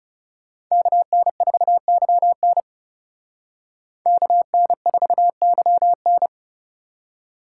Morse code, heard as a keyed tone, about 36 wpm, 700 Hz. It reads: KN4YN KN4YN
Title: Speed Racing - USA Call Signs 30wpm